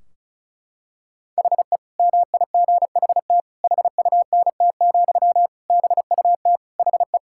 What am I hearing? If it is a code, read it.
Morse code, heard as a keyed tone, about 34 wpm, 700 Hz. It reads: HE MIGHT HUNT, BUT HE
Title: Sentences from Top 1600 Words 35wpm